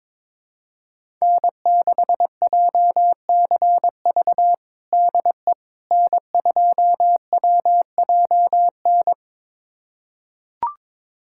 Morse code, heard as a keyed tone, about 22 wpm, 700 Hz. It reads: N6JCV DE N2WJN
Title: USA Callsign DE Callsign - 5 Characters - 22wpm